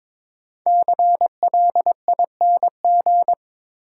Morse code, heard as a keyed tone, about 22 wpm, 690 Hz. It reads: CLING
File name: 5 Letter Words - First 3 Letters Given - 22wpm